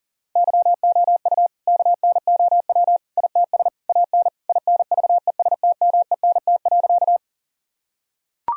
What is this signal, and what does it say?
Morse code, heard as a keyed tone, about 40 wpm, 700 Hz. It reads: YOU KNOW ITS AN INVESTMENT.
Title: Sentences from Top 1000 Words - Review Part 4 - 40wpm